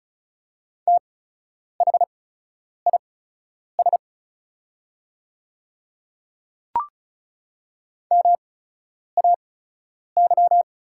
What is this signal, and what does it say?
Morse code, heard as a keyed tone, about 36 wpm, spaced as for 10 wpm, 700 Hz. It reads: THIS MAY
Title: 8x Character Spacing - Top 100 Words - 35wpm